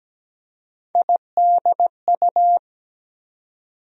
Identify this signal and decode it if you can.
Morse code, heard as a keyed tone, about 17 wpm, 700 Hz. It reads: IDU
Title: USA Call Sign Postfixes - 3 Letters 17wpm